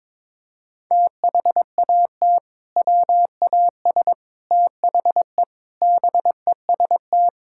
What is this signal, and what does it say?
Morse code, heard as a keyed tone, about 22 wpm, 700 Hz. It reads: THAT WAS THE BEST
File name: practice / 22wpm / Sentences with 3-Word Phrases - 22wpm